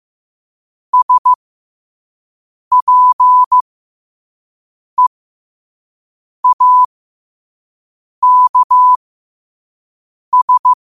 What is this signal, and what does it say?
Morse code, heard as a keyed tone, about 15 wpm, spaced as for 5 wpm, 1000 Hz. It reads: SPEAKS
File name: qotdfiles / 5wpm